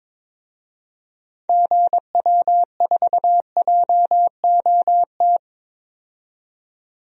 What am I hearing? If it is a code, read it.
Morse code, heard as a keyed tone, about 22 wpm, 700 Hz. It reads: GW4JOT
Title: DXCC-100 Call Signs - 6 Characters - 22wpm